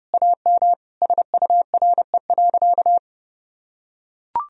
Morse code, heard as a keyed tone, about 30 wpm, 700 Hz. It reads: AM SURE.
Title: Sentences from Top 1000 Words - Review Part 3 - 30wpm